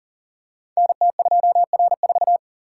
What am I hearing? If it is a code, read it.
Morse code, heard as a keyed tone, about 41 wpm, 700 Hz. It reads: NT2R4
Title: Code Groups - 5 Letter:Numbers - 40wpm